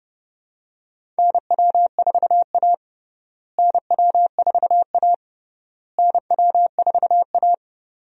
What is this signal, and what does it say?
Morse code, heard as a keyed tone, about 31 wpm, 700 Hz. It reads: NW4A NW4A NW4A
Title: USA Call Signs - 4 Characters - Repeated 3x - 30wpm